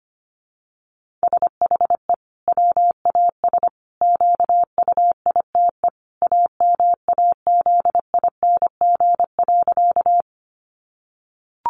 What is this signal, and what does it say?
Morse code, heard as a keyed tone, about 25 wpm, 700 Hz. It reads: SHE WAS QUITE AMAZING.
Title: Sentences from Top 1500 Words 25wpm